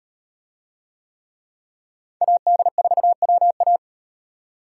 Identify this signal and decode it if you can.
Morse code, heard as a keyed tone, about 34 wpm, 700 Hz. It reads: AD4WA